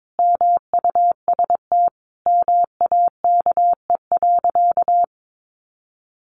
Morse code, heard as a keyed tone, about 22 wpm, 700 Hz. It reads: MUST MAKE.
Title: Sentences from Top 1000 Words 22wpm